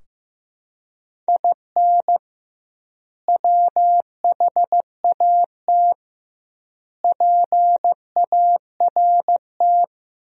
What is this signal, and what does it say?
Morse code, heard as a keyed tone, about 15 wpm, 700 Hz. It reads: IN WHAT PART
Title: 2x Word-Spacing - Sentences from Top 200 Words - 15wpm